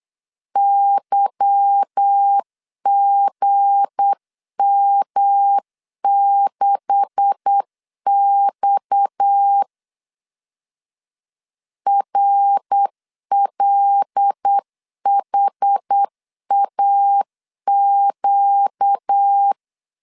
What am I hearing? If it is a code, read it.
Morse code, heard as a keyed tone, about 8 wpm, 800 Hz. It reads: YGM6X RLHAQ
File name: LEZIONI / 9 286 8wpm 3425